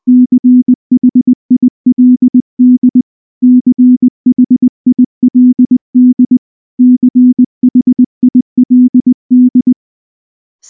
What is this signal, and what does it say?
Morse code, heard as a keyed tone, about 20 wpm, 250 Hz. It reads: CHILD CHILD CHILD
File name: morse / morse-training-20WPM